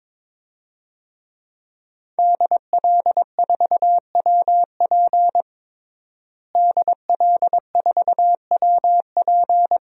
Morse code, heard as a keyed tone, about 21 wpm, 700 Hz. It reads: DL4WP DL4WP
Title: DXCC-100 Call Signs - 5 Characters - Repeated 2x - 22wpm